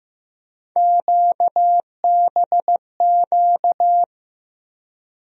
Morse code, heard as a keyed tone, about 15 wpm, 700 Hz. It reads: QBQ
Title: USA Call Sign Postfixes - 3 Letters 15wpm